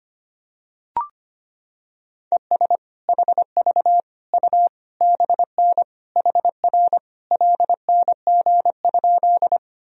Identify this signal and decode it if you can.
Morse code, heard as a keyed tone, about 25 wpm, 700 Hz. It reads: ES HV U BN HR LNG?
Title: Sentences from Top 200 Words and Abbreviations 25wpm